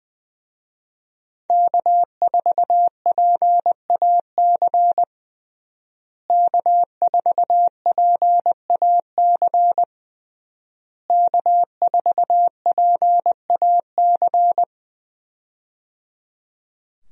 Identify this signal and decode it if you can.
Morse code, heard as a keyed tone, about 20 wpm, 700 Hz. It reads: K4PAC K4PAC K4PAC
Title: USA Call Signs - 5 Characters - Repeated 3x - 20wpm